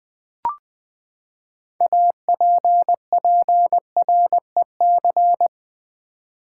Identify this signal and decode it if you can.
Morse code, heard as a keyed tone, about 20 wpm, 700 Hz. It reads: APPREC
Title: Build A Word - 6 Letters At A Time - 12-Letter Words - 20wpm